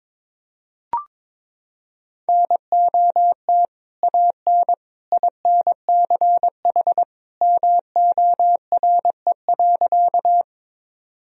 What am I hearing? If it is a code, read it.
Morse code, heard as a keyed tone, about 22 wpm, 700 Hz. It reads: NOT AN INCH MORE.
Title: Sentences from Top 1000 Words - Review Part 3 - 22wpm